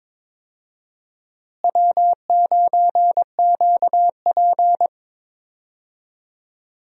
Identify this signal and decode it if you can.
Morse code, heard as a keyed tone, about 22 wpm, 700 Hz. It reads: W9QP
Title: USA Call Signs - 4 Characters - Repeated 3x - 22wpm